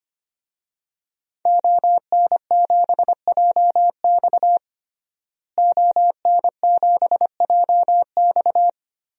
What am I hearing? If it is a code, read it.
Morse code, heard as a keyed tone, about 25 wpm, 700 Hz. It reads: ON7JX ON7JX
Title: DXCC-100 Call Signs - 5 Characters - Repeated 3x - 25wpm